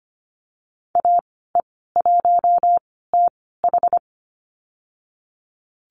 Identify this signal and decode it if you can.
Morse code, heard as a keyed tone, about 25 wpm, spaced as for 16 wpm, 700 Hz. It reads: AE1TH
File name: Lesson 15g - Number 5 - Call Signs 25wpm